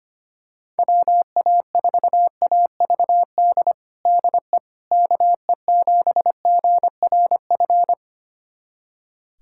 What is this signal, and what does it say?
Morse code, heard as a keyed tone, about 25 wpm, 700 Hz. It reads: WA4AVD DE KE7GRF
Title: USA Callsign DE Callsign 25wpm